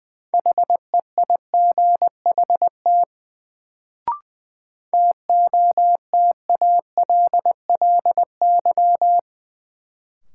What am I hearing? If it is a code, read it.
Morse code, heard as a keyed tone, about 20 wpm, 700 Hz. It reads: HEIGHT TOTALLY
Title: Sentences from Top 1600 Words - New Word Drill Alt - 20wpm